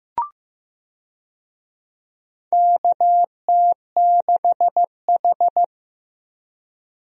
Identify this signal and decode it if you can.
Morse code, heard as a keyed tone, about 15 wpm, 700 Hz. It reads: KT6H
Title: USA Call Signs - 4 Characters - Repeated 2x - 15wpm